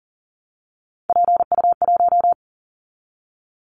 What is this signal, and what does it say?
Morse code, heard as a keyed tone, about 38 wpm, 700 Hz. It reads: PU1
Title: Callsign Prefixes for Top 50 DXCC - 40wpm